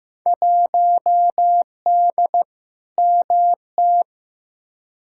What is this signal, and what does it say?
Morse code, heard as a keyed tone, about 15 wpm, 700 Hz. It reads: 1D MT
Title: Field Day Exchanges - 15wpm